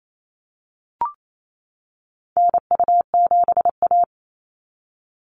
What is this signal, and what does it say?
Morse code, heard as a keyed tone, about 28 wpm, 700 Hz. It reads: NU7A
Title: NA SOTA Activator Callsigns 28wpm